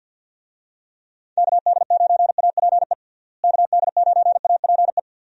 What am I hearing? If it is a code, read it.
Morse code, heard as a keyed tone, about 51 wpm, 700 Hz. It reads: KD9APE KD9APE
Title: USA Call Signs - 6 Characters - Repeated 2x - 50wpm